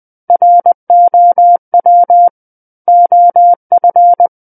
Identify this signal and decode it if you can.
Morse code, heard as a keyed tone, about 20 wpm, 700 Hz. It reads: ROW OF
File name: 20 WPM